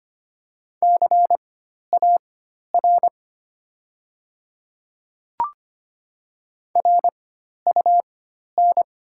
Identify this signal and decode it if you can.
Morse code, heard as a keyed tone, about 25 wpm, spaced as for 12 wpm, 690 Hz. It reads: CAR RUN
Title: 4x Character Spacing - Top 300 Words - 25wpm